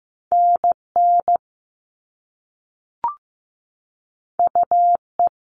Morse code, heard as a keyed tone, about 15 wpm, 700 Hz. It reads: NN UE
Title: All Cut Numbers - 2-Digit 15wpm